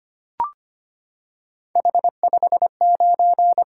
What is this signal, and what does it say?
Morse code, heard as a keyed tone, about 25 wpm, 700 Hz. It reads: H59
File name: Three Letter-Number Combinations - 25wpm